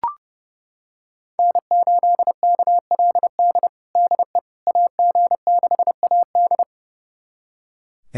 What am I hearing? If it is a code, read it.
Morse code, heard as a keyed tone, about 30 wpm, 700 Hz. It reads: N8KLD DE AG6AD
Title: USA Callsign DE Callsign - 5 Characters - 30wpm